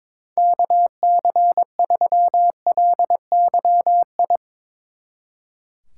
Morse code, heard as a keyed tone, about 23 wpm, 700 Hz. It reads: KC3LYI
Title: Speed Racing - USA 2x3 Call Signs 22wpm